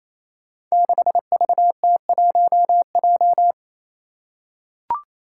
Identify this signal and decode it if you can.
Morse code, heard as a keyed tone, about 27 wpm, 700 Hz. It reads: 6VT1J
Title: Five Challenging Letter-Number Combinations 28wpm